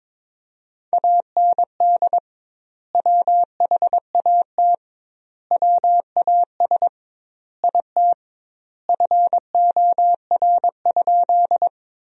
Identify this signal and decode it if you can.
Morse code, heard as a keyed tone, about 22 wpm, 700 Hz. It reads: AND WHAT WAS IT FOR?
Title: 2x Word-Spacing - Sentences from Top 100 Words - 22wpm